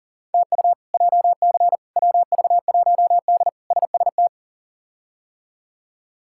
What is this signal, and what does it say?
Morse code, heard as a keyed tone, about 37 wpm, 700 Hz. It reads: TU JC WV1D SST
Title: K1USN SST QSO Practice - 40wpm